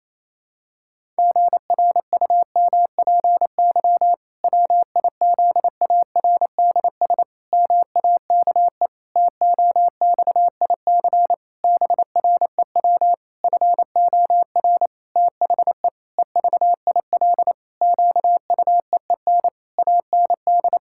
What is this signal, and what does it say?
Morse code, heard as a keyed tone, about 28 wpm, 700 Hz. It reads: GRUMPY WIZARDS MAKE TOXIC BREW FOR THE EVIL QUEEN AND
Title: Pangram Challenge v2 - 28wpm